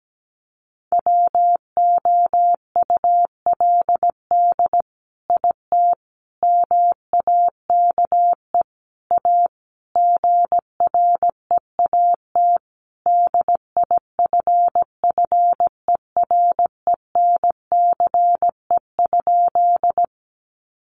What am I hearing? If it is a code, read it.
Morse code, heard as a keyed tone, about 17 wpm, 700 Hz. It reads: WOULD IT MAKE A GREAT DIFFERENCE?
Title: Sentences from Top 800 Words 17wpm